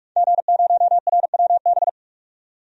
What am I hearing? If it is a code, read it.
Morse code, heard as a keyed tone, about 41 wpm, 700 Hz. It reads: G0RWB
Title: DXCC-100 Call Signs - 5 Characters - 45wpm